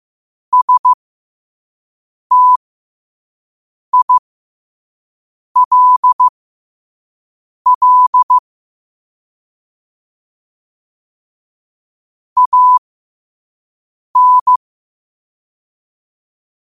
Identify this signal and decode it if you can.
Morse code, heard as a keyed tone, about 15 wpm, spaced as for 5 wpm, 1000 Hz. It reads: STILL AN
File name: qotdfiles / 5wpm